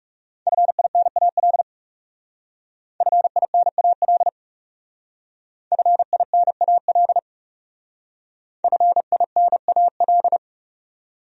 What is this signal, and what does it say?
Morse code, heard as a keyed tone, about 37 wpm, 700 Hz. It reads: FINAL FINAL FINAL FINAL
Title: Speed-Racing Top 500 Words 30wpm